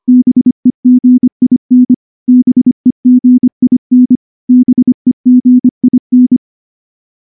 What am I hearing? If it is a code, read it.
Morse code, heard as a keyed tone, about 24 wpm, 250 Hz. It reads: BEGIN BEGIN BEGIN
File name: morse / morse-training-25WPM